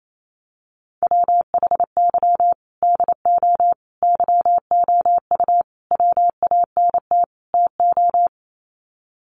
Morse code, heard as a keyed tone, about 28 wpm, 700 Hz. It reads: WHY DO YOU WANT TO
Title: Sets of 5 Words Encore2 28wpm